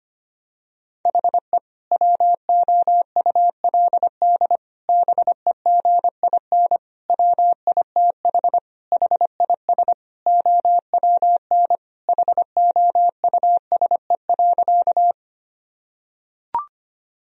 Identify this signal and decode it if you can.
Morse code, heard as a keyed tone, about 25 wpm, 700 Hz. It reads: HE WOULD BEGIN WITH HIS OWN HOUSE.
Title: Sentences from Top 1000 Words - Review Part 1 - 25wpm